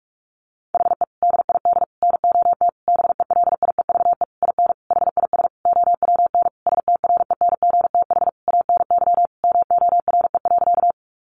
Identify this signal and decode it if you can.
Morse code, heard as a keyed tone, about 44 wpm, 700 Hz. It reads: HE DID NOT BELIEVE IN HIS OWN STRENGTH ANY MORE.
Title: Sentences from Top 1300 Words 45wpm